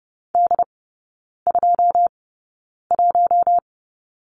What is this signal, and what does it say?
Morse code, heard as a keyed tone, about 30 wpm, spaced as for 9 wpm, 700 Hz. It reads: D21
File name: Lesson 18e - Letter m - Three Characters 30wpm